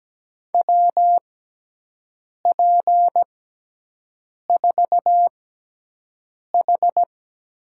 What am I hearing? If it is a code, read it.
Morse code, heard as a keyed tone, about 17 wpm, spaced as for 6 wpm, 700 Hz. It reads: WP4H
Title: USA Call Signs - 5 Characters - 6x Character Spacing - 17wpm